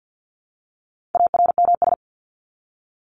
Morse code, heard as a keyed tone, about 48 wpm, 700 Hz. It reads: ARMS
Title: ES:S Suffix - Top 100 Words with Base Words 50wpm